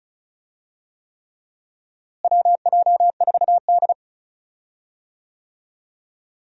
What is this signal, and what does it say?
Morse code, heard as a keyed tone, about 34 wpm, 700 Hz. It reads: WJ4D
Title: USA Call Signs - 4 Characters - Repeated 3x - 35wpm